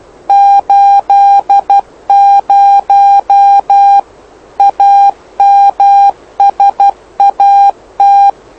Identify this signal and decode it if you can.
Morse code, heard as a keyed tone, about 12 wpm, 800 Hz. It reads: 80 AMSAT
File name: lu4aao / lu7aa 09dic2011 exp planeador vuelo1